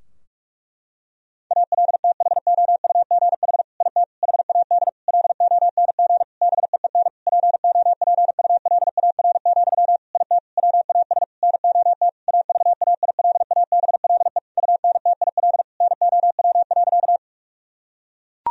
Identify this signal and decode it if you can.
Morse code, heard as a keyed tone, about 41 wpm, 700 Hz. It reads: ALTHOUGH IT HAD LONG BEEN POPULAR, IT WAS NOT AVAILABLE UNTIL NOW.
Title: Sentences from Top 1000 Words - Review Part 3 - 45wpm